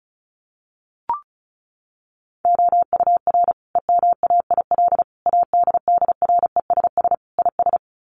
Morse code, heard as a keyed tone, about 34 wpm, 700 Hz. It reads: OUR EMAIL ADDRESS IS
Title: Sets of 5 Words Encore2 35wpm